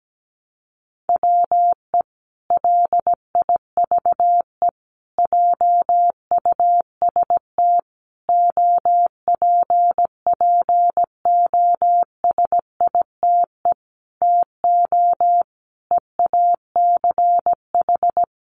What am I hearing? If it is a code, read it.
Morse code, heard as a keyed tone, about 17 wpm, 700 Hz. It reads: WE LIVE JUST OPPOSITE TO EACH
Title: Sentences from Top 1300 Words 17wpm